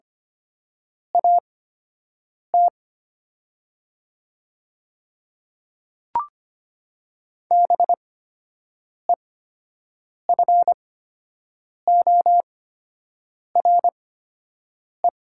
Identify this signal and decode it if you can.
Morse code, heard as a keyed tone, about 25 wpm, spaced as for 7 wpm, 700 Hz. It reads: AT BEFORE
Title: 8x Character Spacing - Top 500 Words - 25wpm